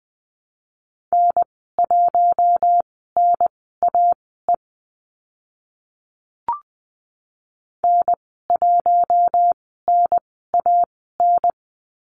Morse code, heard as a keyed tone, about 20 wpm, spaced as for 14 wpm, 700 Hz. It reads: N1NAE N1NAN
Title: 048 Lesson 9g - Number 4 - Call Signs 20wpm